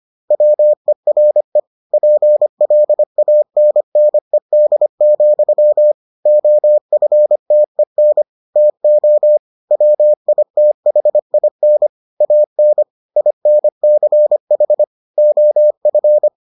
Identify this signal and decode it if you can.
Morse code, heard as a keyed tone, about 25 wpm, 600 Hz. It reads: WERE PLANNED, OFTEN TO WITHIN AN INCH OF